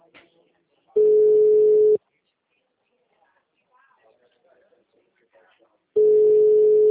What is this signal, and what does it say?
Ringback tone (Europe, 425 Hz), with speech.
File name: out- -2009- - - .6651